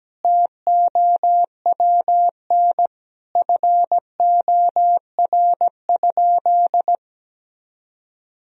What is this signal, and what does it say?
Morse code, heard as a keyed tone, about 17 wpm, 700 Hz. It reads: TOWN FOR?
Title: Sentences from Top 1000 Words - Review Part 2 - 17wpm